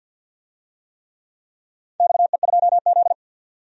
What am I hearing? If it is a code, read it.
Morse code, heard as a keyed tone, about 51 wpm, 700 Hz. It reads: XE2Z
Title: DXCC-100 Call Signs - 4 Characters - 50wpm